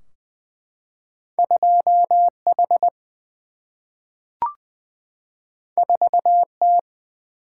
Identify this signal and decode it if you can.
Morse code, heard as a keyed tone, about 20 wpm, 700 Hz. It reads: 2H 4T
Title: Single-Letter Single-Number 20wpm